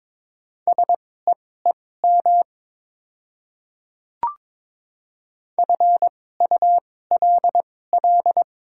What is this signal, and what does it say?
Morse code, heard as a keyed tone, about 22 wpm, spaced as for 16 wpm, 700 Hz. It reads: SEEM FULL
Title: 2x Character Spacing - Top 500 Words - 22wpm